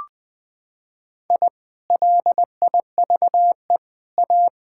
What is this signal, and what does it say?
Morse code, heard as a keyed tone, about 19 wpm, 700 Hz. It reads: I LIVE A
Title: Sentences from Top 1000 Words - Review Part 6 - 20wpm